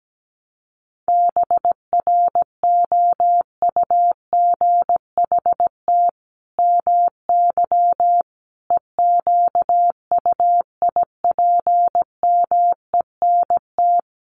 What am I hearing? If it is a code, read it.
Morse code, heard as a keyed tone, about 17 wpm, 700 Hz. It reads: BROUGHT MY EQUIPMENT
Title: Sentences from Top 700 Words 17wpm